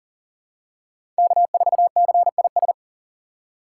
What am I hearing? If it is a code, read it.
Morse code, heard as a keyed tone, about 40 wpm, 700 Hz. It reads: K4CIS
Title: USA Call Signs - 5 Characters - 6x Character Spacing - 40wpm